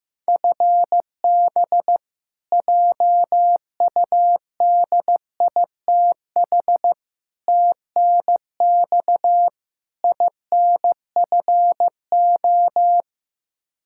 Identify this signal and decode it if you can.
Morse code, heard as a keyed tone, about 15 wpm, 700 Hz. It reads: FB JUDITH TNX INFO
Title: K1USN SST QSO Practice 15wpm